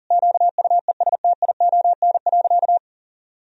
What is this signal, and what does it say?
Morse code, heard as a keyed tone, about 37 wpm, 700 Hz. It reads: QUESTION.
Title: Sentences from Top 1000 Words - Review Part 3 - 40wpm